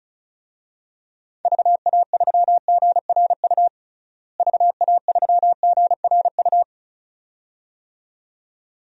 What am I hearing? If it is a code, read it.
Morse code, heard as a keyed tone, about 36 wpm, 700 Hz. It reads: VA3GRU VA3GRU
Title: DXCC-100 Call Signs - 6 Characters - Repeated 2x - 35wpm